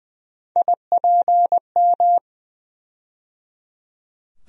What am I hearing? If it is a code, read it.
Morse code, heard as a keyed tone, about 20 wpm, 700 Hz. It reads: IPM
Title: USA Call Sign Postfixes - 3 Letters 20wpm